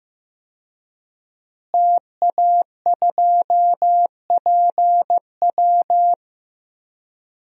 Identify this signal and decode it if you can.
Morse code, heard as a keyed tone, about 15 wpm, 700 Hz. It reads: TA2PW
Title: DXCC-100 Call Signs - 5 Characters - Repeated 2x - 15wpm